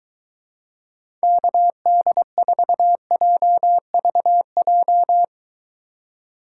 Morse code, heard as a keyed tone, about 23 wpm, 700 Hz. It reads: KD4JVJ